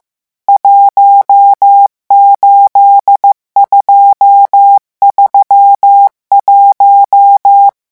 Morse code, heard as a keyed tone, about 15 wpm, 800 Hz. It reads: 18231
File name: morse mp3 / revision11 chiffres vitesse 14